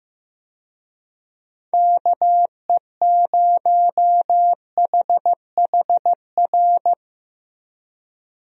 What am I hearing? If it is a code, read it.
Morse code, heard as a keyed tone, about 16 wpm, 700 Hz. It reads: KE0HHR
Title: USA Call Signs - 6 Characters - Repeated 3x - 15wpm